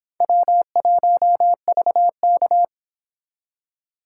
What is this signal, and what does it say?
Morse code, heard as a keyed tone, about 31 wpm, 700 Hz. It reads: W1VK